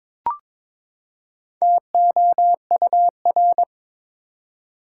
Morse code, heard as a keyed tone, about 22 wpm, 700 Hz. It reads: TOUR
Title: Four Letter Words - 22wpm